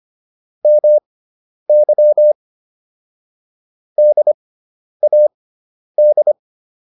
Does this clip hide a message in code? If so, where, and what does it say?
Yes: Morse code from 0.7 to 6.3 s, heard as a keyed tone, about 25 wpm, spaced as for 10 wpm, 600 Hz. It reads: MY DAD